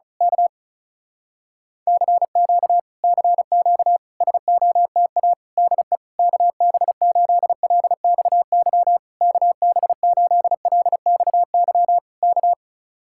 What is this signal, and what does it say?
Morse code, heard as a keyed tone, about 33 wpm, 700 Hz. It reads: K CQ CQ SOTA DE KB8LXY KB8LXY K
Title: Speed-Racing Common QSO Phrases Encore - 35wpm